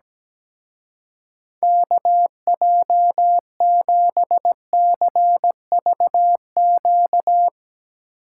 Morse code, heard as a keyed tone, about 19 wpm, 700 Hz. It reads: KJ7CVQ
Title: Speed Racing - USA 2x3 Call Signs 17wpm